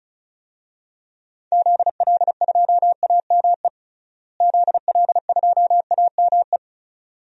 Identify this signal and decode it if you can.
Morse code, heard as a keyed tone, about 34 wpm, 700 Hz. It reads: ZL2AME ZL2AME
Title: DXCC-100 Call Signs - 6 Characters - Repeated 2x - 35wpm